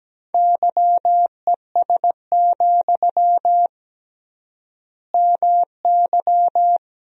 Morse code, heard as a keyed tone, about 17 wpm, 700 Hz. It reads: YES, MY
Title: 3x Word-Spacing - Sentences from Top 500 Words - 17wpm